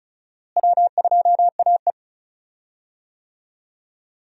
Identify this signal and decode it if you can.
Morse code, heard as a keyed tone, about 36 wpm, 700 Hz. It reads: W2AE
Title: USA Call Signs - 4 Characters - Repeated 2x - 35wpm